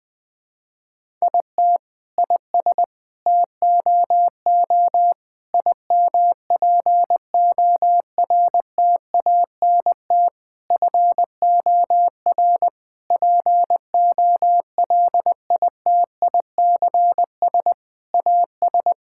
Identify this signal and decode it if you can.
Morse code, heard as a keyed tone, about 20 wpm, 700 Hz. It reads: IT IS TOO IMPORTANT FOR POLITICS AS
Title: Sentences from Top 1000 Words - Review Part 6 - 20wpm